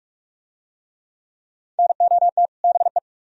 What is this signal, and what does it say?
Morse code, heard as a keyed tone, about 44 wpm, 700 Hz. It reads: NOT BE